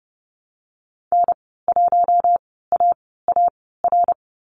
Morse code, heard as a keyed tone, about 30 wpm, spaced as for 17 wpm, 700 Hz. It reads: N1AAR
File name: Lesson 15g - Number 5 - Call Signs 30wpm